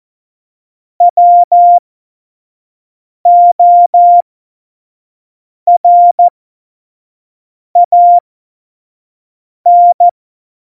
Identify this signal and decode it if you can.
Morse code, heard as a keyed tone, about 14 wpm, spaced as for 5 wpm, 700 Hz. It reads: WORAN
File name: Morsekurs / Witze / Witz 19 Mai 2020, 25 F70